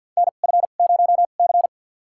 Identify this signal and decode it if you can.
Morse code, heard as a keyed tone, about 50 wpm, spaced as for 36 wpm, 700 Hz. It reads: NF0C
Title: USA Call Signs - 4 Characters - 2x Character Spacing - 50wpm